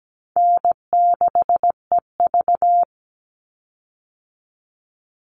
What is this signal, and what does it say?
Morse code, heard as a keyed tone, about 17 wpm, 700 Hz. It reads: N6EV
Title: Easy Call Signs Repeated 3x - 17wpm